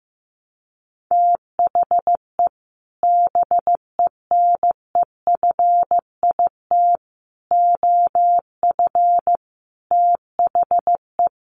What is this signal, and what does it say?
Morse code, heard as a keyed tone, about 15 wpm, 700 Hz. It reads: THE BENEFIT OF THE
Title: Sets of 5 Words Encore 15wpm